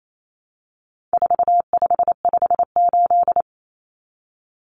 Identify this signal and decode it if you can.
Morse code, heard as a keyed tone, about 28 wpm, 700 Hz. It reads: 4558
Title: Numbers - 4-Digit 28wpm